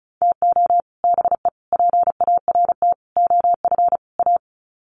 Morse code, heard as a keyed tone, about 34 wpm, 700 Hz. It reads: TO BE PART OF A
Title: Sets of 5 Words Encore2 35wpm